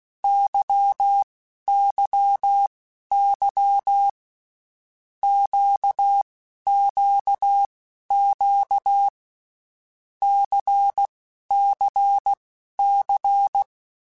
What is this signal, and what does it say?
Morse code, heard as a keyed tone, about 16 wpm, spaced as for 11 wpm, 800 Hz. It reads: YYY QQQ CCC
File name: Groups@10 / Groups@10 mp3s / LFCYQ@10